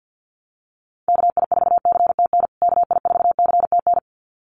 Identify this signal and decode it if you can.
Morse code, heard as a keyed tone, about 50 wpm, 700 Hz. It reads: KI4CTN KI4CTN
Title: USA Call Signs Repeated 2x 50wpm